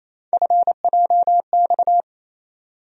Morse code, heard as a keyed tone, about 28 wpm, 700 Hz. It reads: FJX
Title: USA Call Sign Postfixes - 3 Letters 28wpm